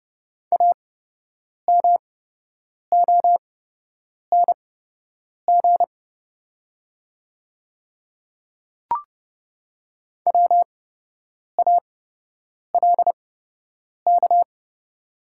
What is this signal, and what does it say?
Morse code, heard as a keyed tone, about 29 wpm, spaced as for 8 wpm, 700 Hz. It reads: AMONG WALK
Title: 8x Character Spacing - Top 500 Words - 30wpm